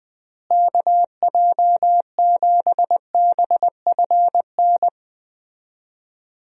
Morse code, heard as a keyed tone, about 20 wpm, 700 Hz. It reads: KJ7BFN
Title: USA Call Signs - 6 Characters - Repeated 3x - 20wpm